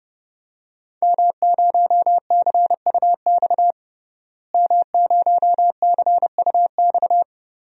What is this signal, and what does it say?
Morse code, heard as a keyed tone, about 30 wpm, 700 Hz. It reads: M0CUX M0CUX
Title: DXCC-100 Call Signs - 5 Characters - Repeated 3x - 30wpm